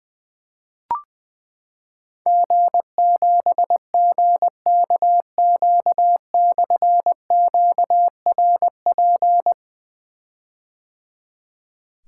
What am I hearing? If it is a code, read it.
Morse code, heard as a keyed tone, about 21 wpm, 700 Hz. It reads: G7GKQ/QRP
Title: Difficult Call Signs 20wpm